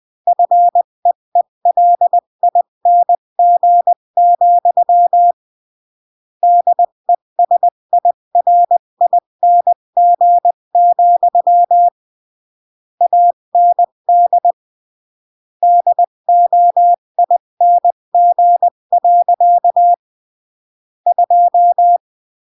Morse code, heard as a keyed tone, about 20 wpm, 700 Hz. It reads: FEELING, DESIRING, AND DOING. 2
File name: CW / jodorowsky / 82 commandments 20-15